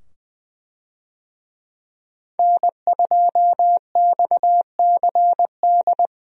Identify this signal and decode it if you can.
Morse code, heard as a keyed tone, about 20 wpm, 700 Hz. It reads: N2XCD
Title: USA Call Signs - 5 Characters - 20wpm